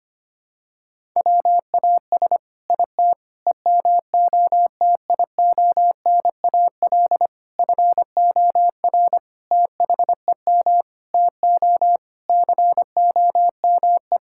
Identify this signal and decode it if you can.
Morse code, heard as a keyed tone, about 25 wpm, 700 Hz. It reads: WAS IT EMOTIONAL FOR THEM TO COME
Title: Sentences from Top 1000 Words - Review Part 5 - 25wpm